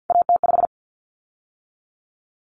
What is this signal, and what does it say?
Morse code, heard as a keyed tone, about 50 wpm, 700 Hz. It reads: AT5
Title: Common Cut Numbers - 3-Digit 50wpm